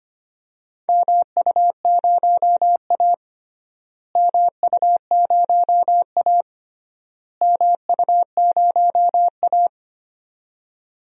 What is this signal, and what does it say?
Morse code, heard as a keyed tone, about 25 wpm, 700 Hz. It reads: MU0A MU0A MU0A
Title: DXCC-100 Call Signs - 4 Characters - Repeated 3x - 25wpm